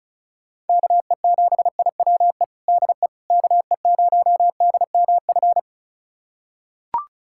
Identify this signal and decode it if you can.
Morse code, heard as a keyed tone, about 36 wpm, 700 Hz. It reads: KE7IWE DE KE0DMF
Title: USA Callsign DE Callsign 35wpm